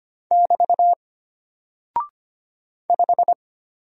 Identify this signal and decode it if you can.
Morse code, heard as a keyed tone, about 25 wpm, 700 Hz. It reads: <BT> 5
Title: Single Character (Letter:Number:Punctuation:Prosign) Alt 25wpm